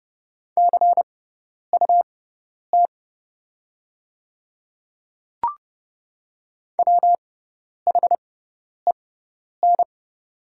Morse code, heard as a keyed tone, about 30 wpm, spaced as for 10 wpm, 700 Hz. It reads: CUT WHEN
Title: 6x Character Spacing - Top 500 Words - 30wpm